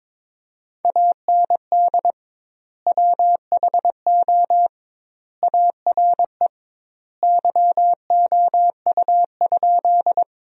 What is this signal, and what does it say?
Morse code, heard as a keyed tone, about 22 wpm, 700 Hz. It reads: AND WHO ARE YOU?
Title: 2x Word-Spacing - Sentences from Top 100 Words - 22wpm